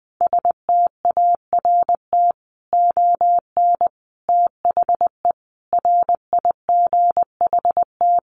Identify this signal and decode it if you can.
Morse code, heard as a keyed tone, about 20 wpm, 700 Hz. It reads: START ON THE RIGHT
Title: Sentences from Top 1400 Words 20wpm